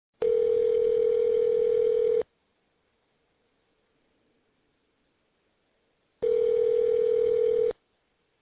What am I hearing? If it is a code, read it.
Ringback tone (North America), 440 + 480 Hz.